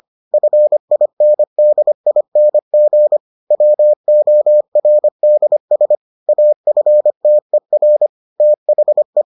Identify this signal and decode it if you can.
Morse code, heard as a keyed tone, about 25 wpm, 600 Hz. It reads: FINDING WORDS AFTER THE